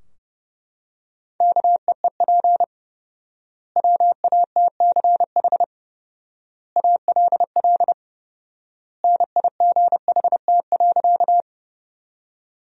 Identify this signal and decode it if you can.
Morse code, heard as a keyed tone, about 30 wpm, 700 Hz. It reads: KEEP WATCH ALL NIGHT.
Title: 4x Word-Spacing - Sentences from Top 300 Words - 30wpm